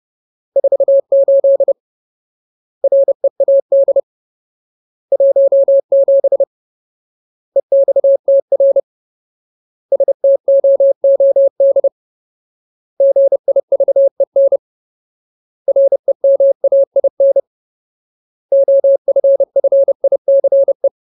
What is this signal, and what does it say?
Morse code, heard as a keyed tone, about 30 wpm, 550 Hz. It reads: 48 READ 17 EXTR STOOD GIVEN REMAIN OFFICE
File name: Ultimate Word List / 30WPM Ultimate